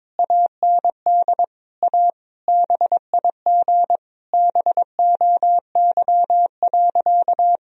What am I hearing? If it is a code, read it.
Morse code, heard as a keyed tone, about 22 wpm, 700 Hz. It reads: AND A BIG BOY.
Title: Sentences from Top 1000 Words - Review Part 1 - 22wpm